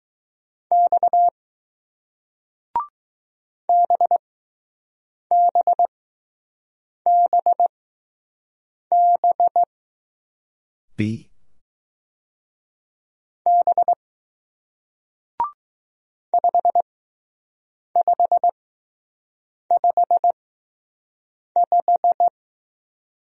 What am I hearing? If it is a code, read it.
Morse code, heard as a keyed tone, about 20 wpm, spaced as for 6 wpm, 700 Hz. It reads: X BBBB B 5555